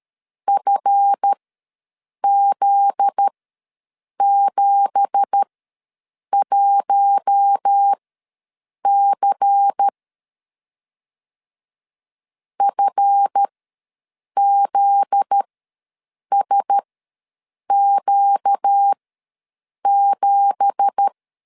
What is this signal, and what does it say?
Morse code, heard as a keyed tone, about 13 wpm, spaced as for 7 wpm, 800 Hz. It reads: FZ71C FZSQ7
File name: LEZIONI / 9 2867 ws 12wpm 3431